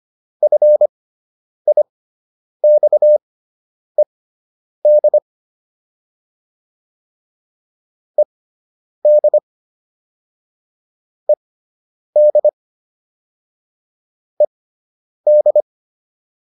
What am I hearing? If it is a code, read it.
Morse code, heard as a keyed tone, about 25 wpm, spaced as for 9 wpm, 600 Hz. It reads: FIXED ED ED ED
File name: ED5 09